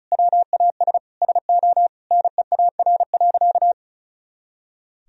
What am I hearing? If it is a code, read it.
Morse code, heard as a keyed tone, about 36 wpm, 700 Hz. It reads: WAS SO NEAR.
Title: Sentences from Top 300 Words and Abbreviations 35wpm